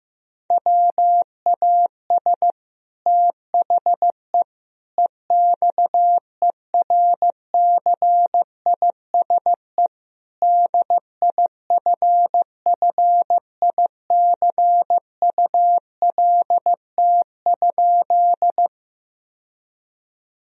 Morse code, heard as a keyed tone, about 15 wpm, 700 Hz. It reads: WAS THE EXERCISE DIFFICULT?
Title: Sentences from Top 1000 Words - Review Part 4 - 15wpm